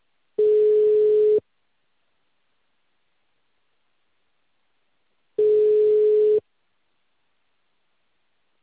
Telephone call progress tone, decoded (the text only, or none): ringback tone